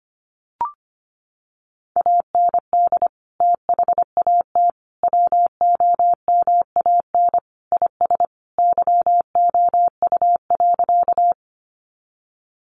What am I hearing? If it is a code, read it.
Morse code, heard as a keyed tone, about 25 wpm, 700 Hz. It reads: AND THAT WOMAN IS YOU.
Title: Sentences from Top 1100 Words 25wpm